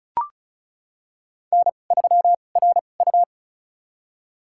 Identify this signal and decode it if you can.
Morse code, heard as a keyed tone, about 34 wpm, spaced as for 25 wpm, 700 Hz. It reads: N3RU
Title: USA Call Signs - 4 Characters - 2x Character Spacing - 35wpm